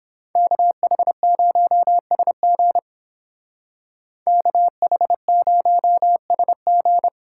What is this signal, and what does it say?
Morse code, heard as a keyed tone, about 28 wpm, 700 Hz. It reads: KH0SG KH0SG